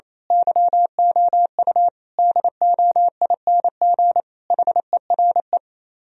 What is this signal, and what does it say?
Morse code, heard as a keyed tone, about 26 wpm, 700 Hz. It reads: YOU DOING HERE
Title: Speed-Racing - Sets of 5 Words 28wpm